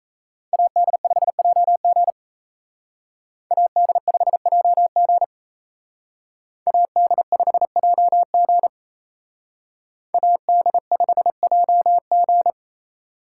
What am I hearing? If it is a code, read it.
Morse code, heard as a keyed tone, about 35 wpm, 700 Hz. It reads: AD5JG AD5JG AD5JG AD5JG